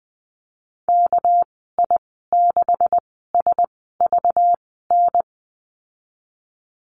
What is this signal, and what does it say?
Morse code, heard as a keyed tone, about 19 wpm, spaced as for 14 wpm, 700 Hz. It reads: KI6SVN
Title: 213 Lesson 34g - Number 8 - Call Signs 20wpm